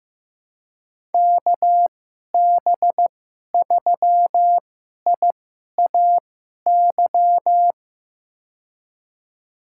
Morse code, heard as a keyed tone, about 15 wpm, spaced as for 11 wpm, 700 Hz. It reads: KB3IAY
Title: USA Call Signs - 6 Characters - 2x Character Spacing - 15wpm